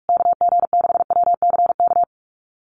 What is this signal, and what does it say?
Morse code, heard as a keyed tone, about 44 wpm, 690 Hz. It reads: KG6WCK
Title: USA Call Signs Repeated 3x 45wpm